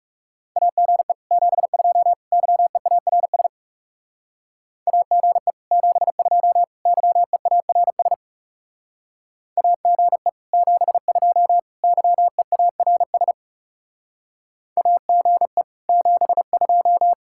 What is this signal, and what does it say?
Morse code, heard as a keyed tone, about 39 wpm, 700 Hz. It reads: AGE 72 YEARS AGE 72 YEARS AGE 72 YEARS AGE 72